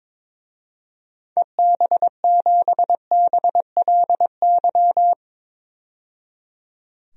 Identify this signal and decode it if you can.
Morse code, heard as a keyed tone, about 21 wpm, 700 Hz. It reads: EB7BLY
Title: DXCC-100 Call Signs - 6 Characters - 22wpm